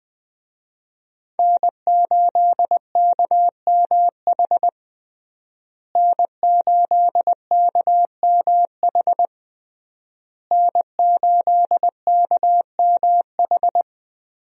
Morse code, heard as a keyed tone, about 20 wpm, 700 Hz. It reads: N8KMH N8KMH N8KMH
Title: USA Call Signs - 5 Characters - Repeated 3x - 20wpm